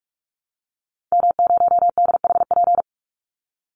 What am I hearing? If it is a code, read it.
Morse code, heard as a keyed tone, about 44 wpm, 700 Hz. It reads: M0DHP
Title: Call Signs Repeated 2x - 45wpm